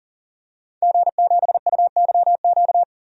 Morse code, heard as a keyed tone, about 40 wpm, 700 Hz. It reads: G7UYQ
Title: DXCC-100 Call Signs - 5 Characters - 40wpm